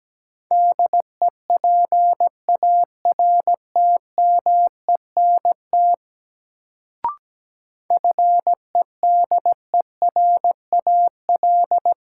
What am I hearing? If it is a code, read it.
Morse code, heard as a keyed tone, about 17 wpm, 700 Hz. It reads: DEPARTMENT FEDERAL
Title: Sentences from Top 1000 Words - Review Part 5 - New Word Drill - 17wpm